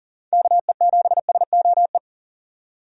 Morse code, heard as a keyed tone, about 38 wpm, 700 Hz. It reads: KE7SOE
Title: USA Call Signs - 6 Characters - 40wpm